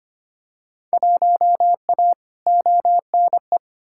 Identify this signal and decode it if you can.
Morse code, heard as a keyed tone, about 25 wpm, 700 Hz. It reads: 1A ONE
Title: Field Day Exchanges - 25wpm